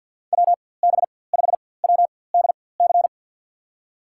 Morse code, heard as a keyed tone, about 53 wpm, spaced as for 24 wpm, 700 Hz. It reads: WB5WDC
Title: USA Call Signs - 6 Characters - 4x Character Spacing - 50wpm